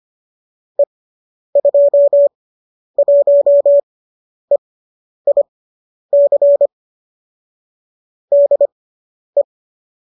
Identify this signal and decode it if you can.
Morse code, heard as a keyed tone, about 25 wpm, spaced as for 10 wpm, 570 Hz. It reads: E21EIC DE
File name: qso005 10